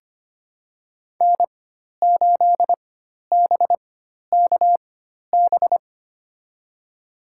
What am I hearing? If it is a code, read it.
Morse code, heard as a keyed tone, about 25 wpm, spaced as for 12 wpm, 700 Hz. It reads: N8BKB
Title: USA Call Signs - 5 Characters - 4x Character Spacing - 25wpm